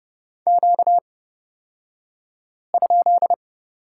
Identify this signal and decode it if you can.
Morse code, heard as a keyed tone, about 31 wpm, spaced as for 5 wpm, 700 Hz. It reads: Q?
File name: Lesson 38d - Letter q - Two Characters 30wpm